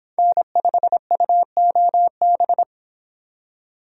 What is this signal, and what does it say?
Morse code, heard as a keyed tone, about 26 wpm, 700 Hz. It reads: N5UOB